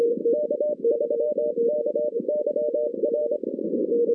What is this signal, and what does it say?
Morse code, heard as a keyed tone, about 25 wpm, 600 Hz, with noise. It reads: K3KYR
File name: 2016 arrl dx cw / 2016-ARRL-DX-CW-K5TR- - 0-41